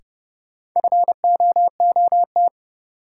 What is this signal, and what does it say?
Morse code, heard as a keyed tone, about 31 wpm, 700 Hz. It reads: FOOT
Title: Sets of 5 Words - New Word Drill - 30wpm